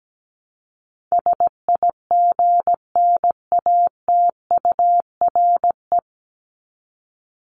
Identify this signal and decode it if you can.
Morse code, heard as a keyed tone, about 17 wpm, 700 Hz. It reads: SIGNATURE
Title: Top 1751-2000 Words 17wpm